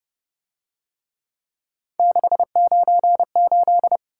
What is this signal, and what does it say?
Morse code, heard as a keyed tone, about 26 wpm, 700 Hz. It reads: 698